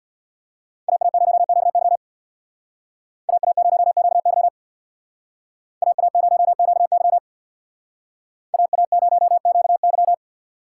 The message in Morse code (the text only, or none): AA0QY AA0QY AA0QY AA0QY